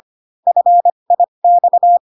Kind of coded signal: Morse code